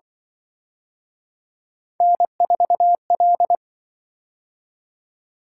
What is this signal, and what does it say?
Morse code, heard as a keyed tone, about 25 wpm, 700 Hz. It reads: N4L
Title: Speed-Racing - Three Letter-Number Combinations - 20wpm